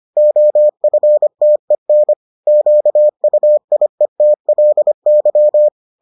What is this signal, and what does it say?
Morse code, heard as a keyed tone, about 25 wpm, 600 Hz. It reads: OFTEN QUIETLY